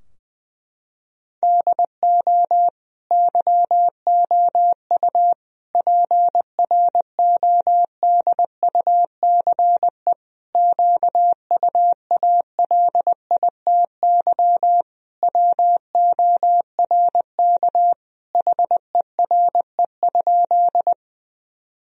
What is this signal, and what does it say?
Morse code, heard as a keyed tone, about 20 wpm, 700 Hz. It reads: DO YOU PRODUCE QUALITY WORK HERE?
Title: Sentences from Top 1000 Words - Review Part 4 - 20wpm